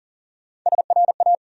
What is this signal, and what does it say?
Morse code, heard as a keyed tone, about 40 wpm, 700 Hz. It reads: SRA
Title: USA Call Sign Postfixes - 3 Letters 40wpm